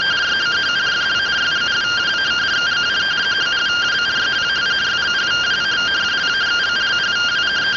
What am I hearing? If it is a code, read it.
Two-tone teleprinter signal, about 170 Hz shift.